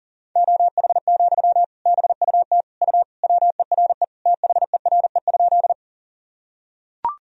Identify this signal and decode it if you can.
Morse code, heard as a keyed tone, about 37 wpm, 700 Hz. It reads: OH, BUT U WERE THERE?
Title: Sentences from Top 500 Words and Abbreviations 40wpm